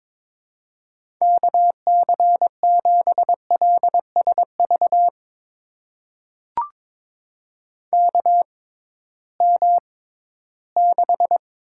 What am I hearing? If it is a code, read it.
Morse code, heard as a keyed tone, about 21 wpm, spaced as for 14 wpm, 700 Hz. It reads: KC7LSV KM6
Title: USA Call Signs - 6 Characters - 6x Character Spacing - 22wpm